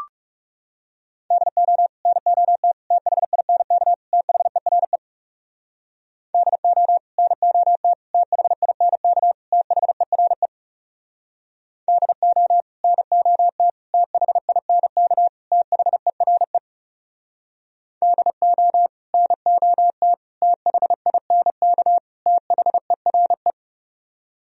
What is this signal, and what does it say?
Morse code, heard as a keyed tone, about 40 wpm, 700 Hz. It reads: DO NOT THINK THERE DO NOT THINK THERE DO NOT THINK THERE DO NOT THINK THERE